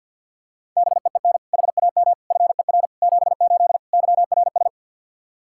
Morse code, heard as a keyed tone, about 51 wpm, 700 Hz. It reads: BEEN HAM FER 78 YRS
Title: K1USN SST QSO Practice 50wpm